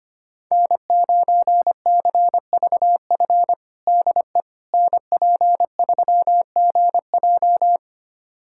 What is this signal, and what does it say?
Morse code, heard as a keyed tone, about 25 wpm, 700 Hz. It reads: N9CVF DE NP3GJ
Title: USA Callsign DE Callsign - 5 Characters - 25wpm